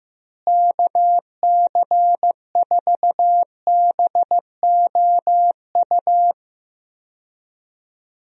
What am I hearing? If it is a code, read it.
Morse code, heard as a keyed tone, about 15 wpm, 700 Hz. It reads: KC4BOU
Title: USA Call Signs - 6 Characters - Repeated 2x - 15wpm